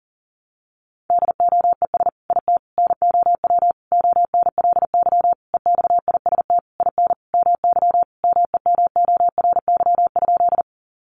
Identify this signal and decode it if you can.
Morse code, heard as a keyed tone, about 40 wpm, 700 Hz. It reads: DOES IT NOW ONLY EXIST IN MY MEMORY?
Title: Sentences from Top 1300 Words 40wpm